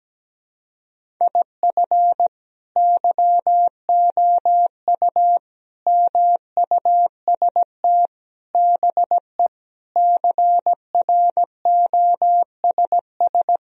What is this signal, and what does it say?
Morse code, heard as a keyed tone, about 17 wpm, 700 Hz. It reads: IF YOU MUST BE CROSS
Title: Sentences from Top 1000 Words - Review Part 1 - 17wpm